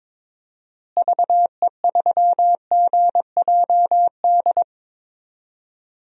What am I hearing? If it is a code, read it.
Morse code, heard as a keyed tone, about 22 wpm, 700 Hz. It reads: VE3GJD
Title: DXCC-100 Call Signs - 6 Characters - Repeated 2x - 22wpm